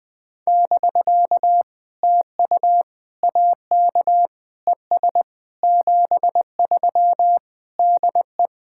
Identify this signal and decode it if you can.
Morse code, heard as a keyed tone, about 20 wpm, 700 Hz. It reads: <BK> TU AK ES 73 DE
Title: POTA QSO Practice - 20wpm